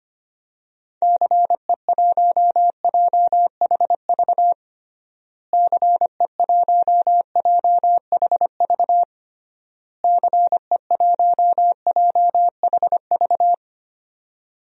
Morse code, heard as a keyed tone, about 25 wpm, 700 Hz. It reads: CE1JHV CE1JHV CE1JHV
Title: DXCC-100 Call Signs - 6 Characters - Repeated 3x - 25wpm